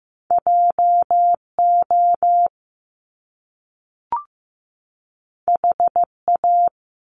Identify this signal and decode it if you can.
Morse code, heard as a keyed tone, about 15 wpm, 700 Hz. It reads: JO HA
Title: Top Two Letter Combinations in Words 15wpm